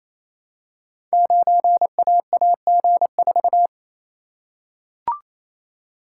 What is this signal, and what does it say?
Morse code, heard as a keyed tone, about 27 wpm, 700 Hz. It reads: 9AAG4
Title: Code Groups - 5 Letter:Numbers - 28wpm